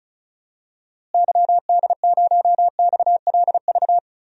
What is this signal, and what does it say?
Morse code, heard as a keyed tone, about 34 wpm, 700 Hz. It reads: YD0XLV
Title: DXCC-100 Call Signs - 6 Characters - Repeated 3x - 35wpm